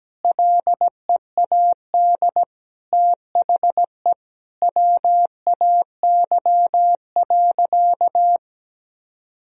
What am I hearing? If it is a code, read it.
Morse code, heard as a keyed tone, about 17 wpm, 700 Hz. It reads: LEAD THE WAY.
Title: Sentences from Top 1000 Words - Review Part 2 - 17wpm